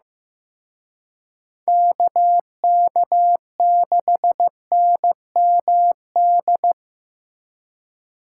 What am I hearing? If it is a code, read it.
Morse code, heard as a keyed tone, about 18 wpm, 700 Hz. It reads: KK6NMD
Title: Speed Racing - USA Call Signs 15wpm